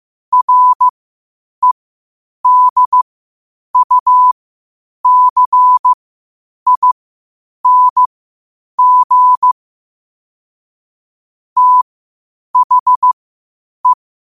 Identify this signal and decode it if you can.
Morse code, heard as a keyed tone, about 15 wpm, spaced as for 8 wpm, 1000 Hz. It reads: REDUCING THE
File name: qotdfiles / 8wpm